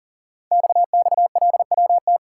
Morse code, heard as a keyed tone, about 39 wpm, 700 Hz. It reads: XXLWT
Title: Code Groups - 5 Letters - 40wpm